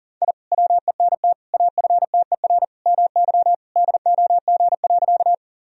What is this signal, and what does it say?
Morse code, heard as a keyed tone, about 40 wpm, 700 Hz. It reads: I WENT AFTER MY DOG.
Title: Sentences from Top 1000 Words - Review Part 2 - 40wpm